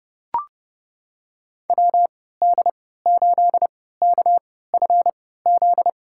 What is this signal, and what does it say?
Morse code, heard as a keyed tone, about 30 wpm, spaced as for 17 wpm, 700 Hz. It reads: WD8KFZ
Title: USA Call Signs - 6 Characters - 3x Character Spacing - 30wpm